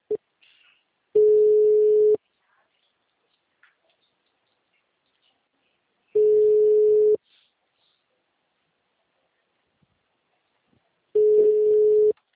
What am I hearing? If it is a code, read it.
Ringback tone (Europe, 425 Hz).